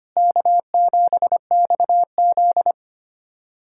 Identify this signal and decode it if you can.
Morse code, heard as a keyed tone, about 25 wpm, 700 Hz. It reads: K7XZ
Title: USA Callsign DE Callsign - 4 Characters - 25wpm